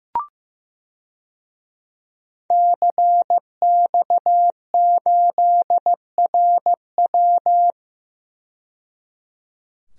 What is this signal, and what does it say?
Morse code, heard as a keyed tone, about 15 wpm, 700 Hz. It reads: CX8RW
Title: DXCC-100 Call Signs - 5 Characters - 15wpm